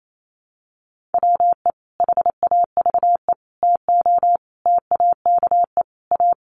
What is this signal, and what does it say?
Morse code, heard as a keyed tone, about 28 wpm, 700 Hz. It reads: WE HAVE TO TAKE A
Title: Sets of 5 Words Encore 28wpm